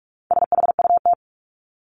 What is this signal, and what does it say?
Morse code, heard as a keyed tone, about 44 wpm, 700 Hz. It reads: SHUT
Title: Top 1751-2000 Words 45wpm